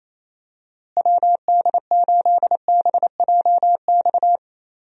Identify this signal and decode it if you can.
Morse code, heard as a keyed tone, about 28 wpm, 700 Hz. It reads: WD8BJX
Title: USA Call Signs - 6 Characters - Repeated 3x - 28wpm